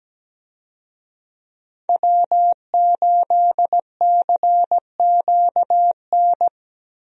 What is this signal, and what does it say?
Morse code, heard as a keyed tone, about 17 wpm, 700 Hz. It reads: W8CQN
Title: USA Call Signs - 5 Characters - 17wpm